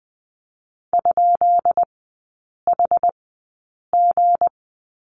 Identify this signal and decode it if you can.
Morse code, heard as a keyed tone, about 20 wpm, spaced as for 8 wpm, 700 Hz. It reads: ?HG
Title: 172 Lesson 28e - Number 9 - Three Characters 20wpm